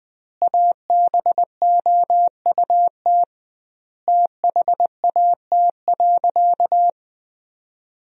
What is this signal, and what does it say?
Morse code, heard as a keyed tone, about 20 wpm, 700 Hz. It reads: ABOUT THAT.
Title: 2x Word-Spacing - Sentences from Top 400 Words - 20wpm